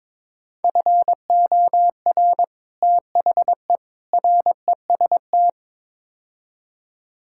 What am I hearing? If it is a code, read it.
Morse code, heard as a keyed tone, about 22 wpm, 700 Hz. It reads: FOR THE REST
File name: practice / 22wpm / Build A Phrase - 5-Word Phrases - 22wpm